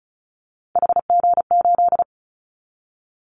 Morse code, heard as a keyed tone, about 36 wpm, 700 Hz. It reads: HG8
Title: Callsign Prefixes for Top 50 DXCC - 35wpm